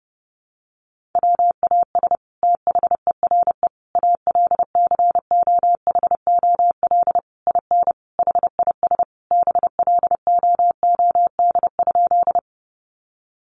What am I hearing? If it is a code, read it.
Morse code, heard as a keyed tone, about 28 wpm, 700 Hz. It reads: WAS THERE ALCOHOL IN HIS BLOOD?
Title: Sentences from Top 1400 Words 30wpm